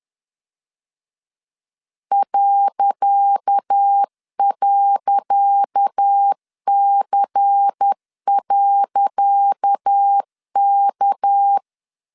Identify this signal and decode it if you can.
Morse code, heard as a keyed tone, about 10 wpm, 800 Hz. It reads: ..C.K
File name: LEZIONI / 10 SIGNS / 10C DOT 10wpm 2940